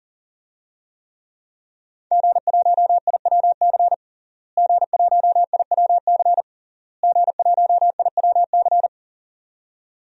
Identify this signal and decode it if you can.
Morse code, heard as a keyed tone, about 40 wpm, 700 Hz. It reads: G1IWC G1IWC G1IWC
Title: DXCC-100 Call Signs - 5 Characters - Repeated 3x - 40wpm